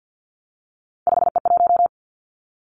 Morse code, heard as a keyed tone, about 50 wpm, 700 Hz. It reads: 5E1
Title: Three Challenging Letter-Number Combinations 50wpm